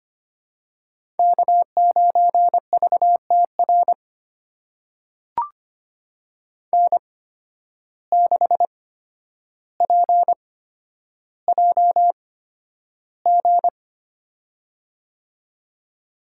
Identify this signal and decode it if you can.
Morse code, heard as a keyed tone, about 25 wpm, spaced as for 11 wpm, 700 Hz. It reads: K9VTR N6PJG
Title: USA Call Signs - 5 Characters - 8x Character Spacing - 25wpm